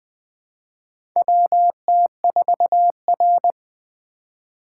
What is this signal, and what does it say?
Morse code, heard as a keyed tone, about 20 wpm, 700 Hz. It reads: WT4R
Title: USA Call Signs - 4 Characters - 4x Character Spacing - 20wpm